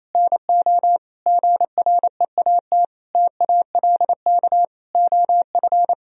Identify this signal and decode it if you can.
Morse code, heard as a keyed tone, about 27 wpm, 700 Hz. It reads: NO GREAT TALK OF
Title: Sentences from Top 1000 Words - Review Part 2 - 28wpm